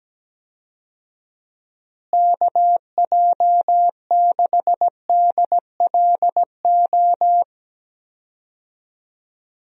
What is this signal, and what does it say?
Morse code, heard as a keyed tone, about 17 wpm, 700 Hz. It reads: KJ6DLO
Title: USA Call Signs - 6 Characters - Repeated 3x - 17wpm